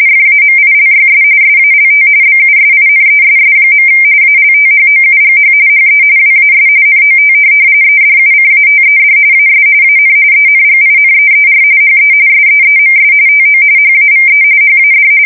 RTTY, about 45.45 baud. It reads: FAMILYS SALT BUSINESS. ALTHOUGH HE WAS NOT EDUCATED AS AN ENGINEER HE SEEMS TO HAVE HAD
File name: fsk170 / teletype-corp